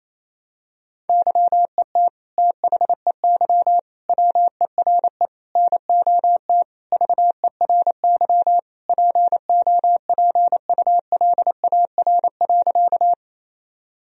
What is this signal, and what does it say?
Morse code, heard as a keyed tone, about 28 wpm, 700 Hz. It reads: YET THEY WERE NOT VERY POPULAR.
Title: Sentences from Top 1000 Words - Review Part 3 - 28wpm